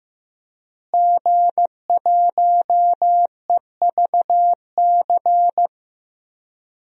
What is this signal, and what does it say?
Morse code, heard as a keyed tone, about 15 wpm, 700 Hz. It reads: G1EVC
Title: DXCC-100 Call Signs - 5 Characters - Repeated 3x - 15wpm